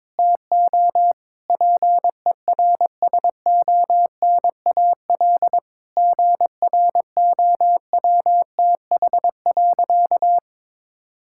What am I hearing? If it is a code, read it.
Morse code, heard as a keyed tone, about 22 wpm, 700 Hz. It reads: TO PERSONAL GROWTH.
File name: Difficult Sentences with Hints - 22wpm